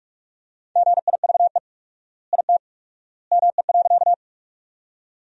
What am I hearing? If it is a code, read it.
Morse code, heard as a keyed tone, about 44 wpm, 700 Hz. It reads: GIVE IT ME.
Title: 4x Word-Spacing - Sentences from Top 200 Words - 45wpm